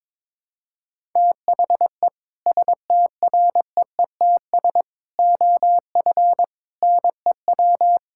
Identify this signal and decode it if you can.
Morse code, heard as a keyed tone, about 22 wpm, 700 Hz. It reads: THE STREETS OF NEW
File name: practice / 22wpm / Sentences with 3-Word Phrases - 22wpm